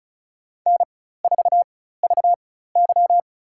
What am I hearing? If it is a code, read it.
Morse code, heard as a keyed tone, about 34 wpm, spaced as for 16 wpm, 700 Hz. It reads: N4VY
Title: USA Call Signs - 4x Character Spacing - 35wpm